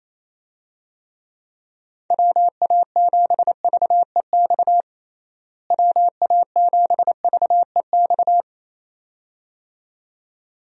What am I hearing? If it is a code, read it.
Morse code, heard as a keyed tone, about 28 wpm, 700 Hz. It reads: WA7VEX WA7VEX
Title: USA Call Signs - 6 Characters - Repeated 2x - 28wpm